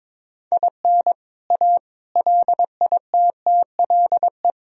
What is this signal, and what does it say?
Morse code, heard as a keyed tone, about 22 wpm, 700 Hz. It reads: IN A LITTLE
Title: Sentences from Top 1000 Words - Review Part 1 - 22wpm